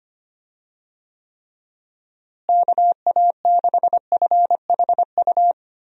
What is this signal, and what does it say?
Morse code, heard as a keyed tone, about 25 wpm, 700 Hz. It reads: KA6FHU
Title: USA Call Signs - 6 Characters - Repeated 3x - 25wpm